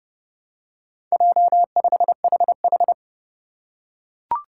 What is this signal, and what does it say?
Morse code, heard as a keyed tone, about 30 wpm, 700 Hz. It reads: J5HH
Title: Four Challenging Letter-Number Combinations 30wpm